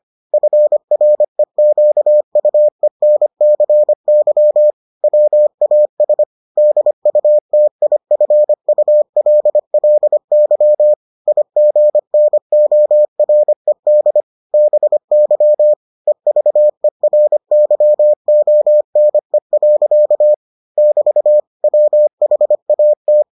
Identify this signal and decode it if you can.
Morse code, heard as a keyed tone, about 25 wpm, 600 Hz. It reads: FREQUENCY WAS DUTIFULLY IGNORED BY EVERYONE. <BT> WHAT